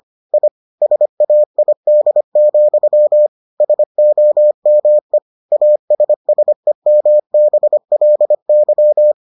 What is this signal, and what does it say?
Morse code, heard as a keyed tone, about 25 wpm, 600 Hz. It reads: I SAID, SOME ASSEMBLY